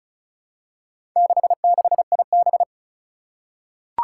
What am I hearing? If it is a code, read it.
Morse code, heard as a keyed tone, about 36 wpm, 700 Hz. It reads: 66IB
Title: Four Challenging Letter-Number Combinations 35wpm